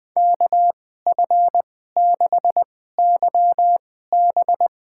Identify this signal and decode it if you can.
Morse code, heard as a keyed tone, about 20 wpm, spaced as for 14 wpm, 700 Hz. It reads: KF6YB
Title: USA Call Signs - 2x Character Spacing - 20wpm